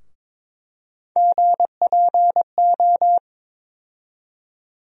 Morse code, heard as a keyed tone, about 22 wpm, 700 Hz. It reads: GPO
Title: USA Call Sign Postfixes - 3 Letters 22wpm